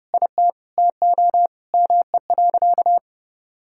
Morse code, heard as a keyed tone, about 32 wpm, 700 Hz. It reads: IT TO ME.
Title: Sentences from Top 1600 Words 30wpm